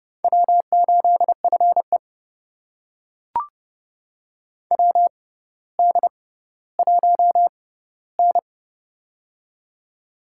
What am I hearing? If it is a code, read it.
Morse code, heard as a keyed tone, about 30 wpm, spaced as for 15 wpm, 700 Hz. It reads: W8FE WD1N
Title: USA Call Signs - 4 Characters - 6x Character Spacing - 30wpm